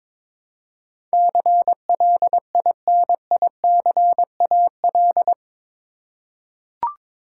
Morse code, heard as a keyed tone, about 22 wpm, 700 Hz. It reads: CLINICAL
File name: practice / 22wpm / Eight Letter Words - 22wpm